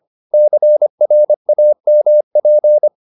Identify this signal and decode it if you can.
Morse code, heard as a keyed tone, about 25 wpm, 600 Hz. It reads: CRAMP
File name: the old man and the sea part 1 25wpm